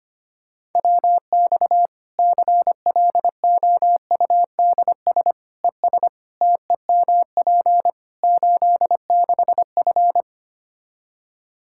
Morse code, heard as a keyed tone, about 25 wpm, 700 Hz. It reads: WX CLOUDS ES TEMP 86F
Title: Practice QSO - Exchange 2 - Info - 25wpm